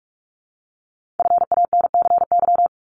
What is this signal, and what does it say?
Morse code, heard as a keyed tone, about 46 wpm, 700 Hz. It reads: FANCY
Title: Multi-letter Phonemes Repeated 3x 45wpm